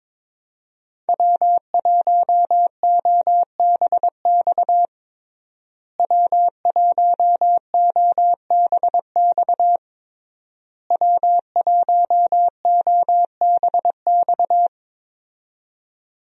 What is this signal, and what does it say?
Morse code, heard as a keyed tone, about 22 wpm, 700 Hz. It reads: W1OBX W1OBX W1OBX
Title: USA Call Signs - 5 Characters - Repeated 3x - 22wpm